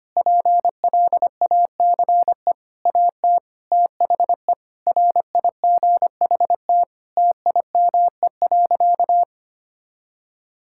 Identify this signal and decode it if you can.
Morse code, heard as a keyed tone, about 25 wpm, 700 Hz. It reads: PLACE AT THE RIGHT TIME.
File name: practice / 25wpm / Sentences with 9-Word Phrases - 25wpm